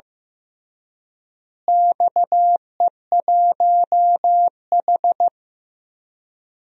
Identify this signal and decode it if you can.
Morse code, heard as a keyed tone, about 15 wpm, 700 Hz. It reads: XE1H
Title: DXCC-100 Call Signs - 4 Characters - Repeated 3x - 15wpm